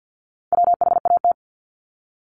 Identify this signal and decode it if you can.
Morse code, heard as a keyed tone, about 50 wpm, 700 Hz. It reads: WHAT
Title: Multi-letter Phonemes 50wpm